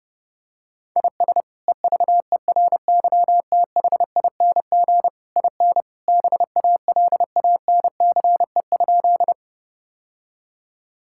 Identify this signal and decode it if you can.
Morse code, heard as a keyed tone, about 30 wpm, 700 Hz. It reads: IS EVERYTHING IN BALANCE?
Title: Sentences from Top 1000 Words - Review Part 6 - 30wpm